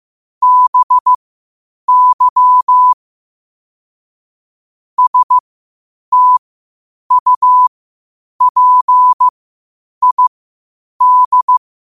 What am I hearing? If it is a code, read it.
Morse code, heard as a keyed tone, about 15 wpm, spaced as for 8 wpm, 1000 Hz. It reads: BY STUPID
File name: 8wpm